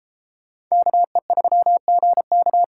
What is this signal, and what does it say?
Morse code, heard as a keyed tone, about 35 wpm, 700 Hz. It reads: KE3GK